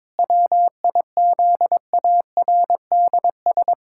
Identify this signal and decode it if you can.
Morse code, heard as a keyed tone, about 22 wpm, 700 Hz. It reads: WIZARDS
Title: Pangram Challenge v2 - 22wpm